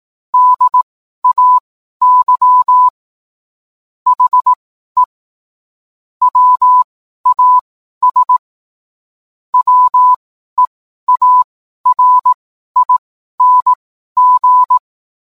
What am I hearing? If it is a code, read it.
Morse code, heard as a keyed tone, about 18 wpm, spaced as for 12 wpm, 1000 Hz. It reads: DAY HE WAS WEARING